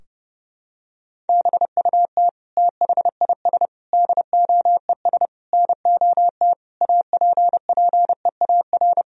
Morse code, heard as a keyed tone, about 30 wpm, 700 Hz. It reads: BUT THIS DOES NOT APPEAR
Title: Sentences from Top 1000 Words - Review Part 2 - 30wpm